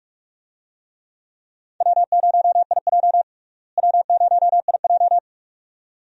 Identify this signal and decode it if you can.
Morse code, heard as a keyed tone, about 44 wpm, 700 Hz. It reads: W0IJ W0IJ
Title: USA Call Signs - 4 Characters - Repeated 2x - 45wpm